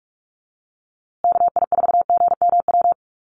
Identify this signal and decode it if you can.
Morse code, heard as a keyed tone, about 44 wpm, 700 Hz. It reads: KI4GMW
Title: USA Call Signs Repeated 3x 45wpm